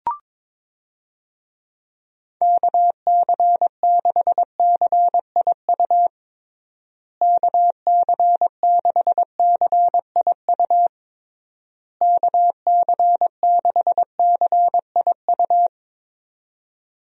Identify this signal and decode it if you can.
Morse code, heard as a keyed tone, about 22 wpm, 700 Hz. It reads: KC6CIU KC6CIU KC6CIU
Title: USA Call Signs - 6 Characters - Repeated 3x - 22wpm